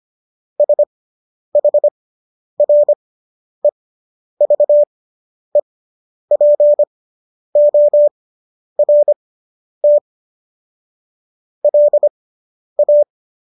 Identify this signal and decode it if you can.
Morse code, heard as a keyed tone, about 25 wpm, spaced as for 10 wpm, 590 Hz. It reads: SHREVEPORT LA